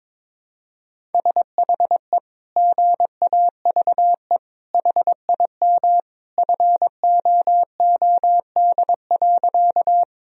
Morse code, heard as a keyed tone, about 22 wpm, 700 Hz. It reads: SHE GAVE HIM FOOD.
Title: Sentences from Top 1000 Words - Review Part 1 - 22wpm